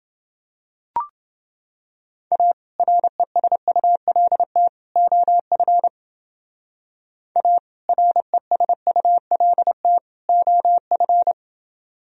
Morse code, heard as a keyed tone, about 29 wpm, 700 Hz. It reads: A RESULT OF A RESULT OF